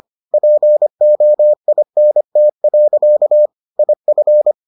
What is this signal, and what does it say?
Morse code, heard as a keyed tone, about 25 wpm, 600 Hz. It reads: POINT. IF